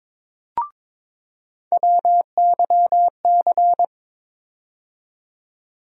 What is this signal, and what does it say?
Morse code, heard as a keyed tone, about 21 wpm, 700 Hz. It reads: WYC
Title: USA Call Sign Postfixes - 3 Letters 22wpm